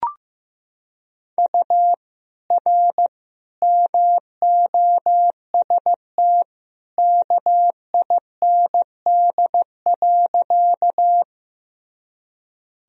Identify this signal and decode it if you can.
Morse code, heard as a keyed tone, about 16 wpm, 700 Hz. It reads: U R MOST KIND.
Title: Sentences from Top 200 Words and Abbreviations 15wpm